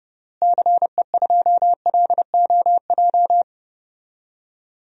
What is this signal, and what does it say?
Morse code, heard as a keyed tone, about 32 wpm, 700 Hz. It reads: CE2LOJ
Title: DXCC-100 Call Signs - 6 Characters - Repeated 3x - 30wpm